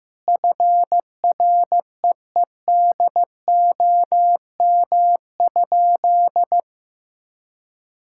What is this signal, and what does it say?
Morse code, heard as a keyed tone, about 16 wpm, 700 Hz. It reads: FREEDOM?
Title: Sentences from Top 1000 Words - Review Part 5 - 15wpm